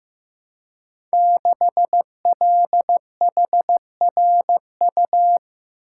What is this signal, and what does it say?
Morse code, heard as a keyed tone, about 15 wpm, 700 Hz. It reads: 6LHRU
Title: Code Groups - 5 Letter:Numbers - 15wpm